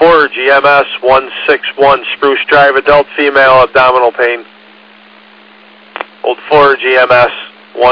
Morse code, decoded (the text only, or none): none